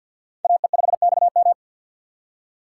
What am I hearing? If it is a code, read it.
Morse code, heard as a keyed tone, about 52 wpm, 700 Hz. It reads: AE5XM
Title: USA Call Signs - 2x Character Spacing - 50wpm